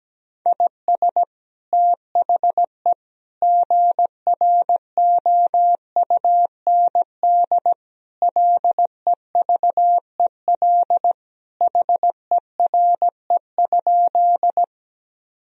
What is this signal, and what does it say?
Morse code, heard as a keyed tone, about 17 wpm, 700 Hz. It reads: IS THE GROUND LEVEL HERE?
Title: Sentences from Top 1000 Words - Review Part 4 - 17wpm